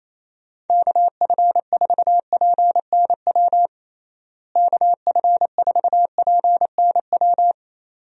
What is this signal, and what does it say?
Morse code, heard as a keyed tone, about 28 wpm, 700 Hz. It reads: KF4PNW KF4PNW
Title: USA Call Signs - 6 Characters - Repeated 2x - 28wpm